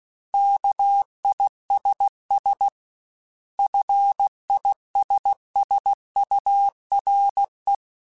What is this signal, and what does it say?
Morse code, heard as a keyed tone, about 16 wpm, 800 Hz. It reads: KISS FISSURE
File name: General / General-mp3s / ISS @15